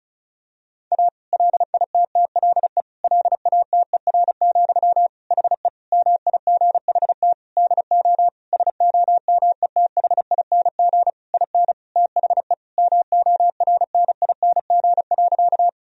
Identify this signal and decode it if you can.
Morse code, heard as a keyed tone, about 36 wpm, 700 Hz. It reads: A LITTLE LATER, HE MIGHT DO SOMETHING IN THE MORNING.
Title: Sentences from Top 1000 Words - Review Part 4 - 35wpm